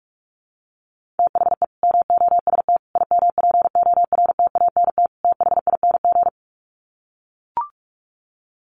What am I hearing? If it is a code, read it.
Morse code, heard as a keyed tone, about 44 wpm, 700 Hz. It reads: THE MOST IMPORTANT THING
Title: Sets of 4 Words 45wpm